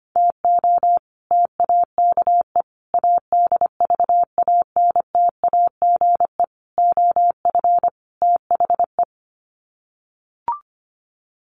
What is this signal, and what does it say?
Morse code, heard as a keyed tone, about 25 wpm, 700 Hz. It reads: TO TAKE ADVANTAGE OF THE
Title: Sets of 5 Words 25wpm